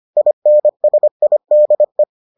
Morse code, heard as a keyed tone, about 25 wpm, 600 Hz. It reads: INSIDE